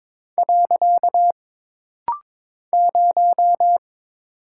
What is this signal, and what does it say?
Morse code, heard as a keyed tone, about 22 wpm, 700 Hz. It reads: . 0
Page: This rapid-fire timing is geared to push your ICR (Instant Character Recognition) skills.